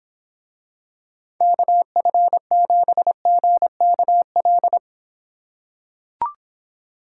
Morse code, heard as a keyed tone, about 26 wpm, 700 Hz. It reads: KF7GKL